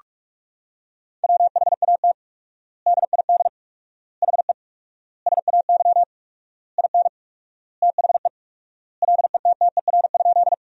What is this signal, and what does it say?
Morse code, heard as a keyed tone, about 46 wpm, 700 Hz. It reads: WHAT DID HE SAY IN THE LETTER?
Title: 4x Word-Spacing - Sentences from Top 300 Words - 45wpm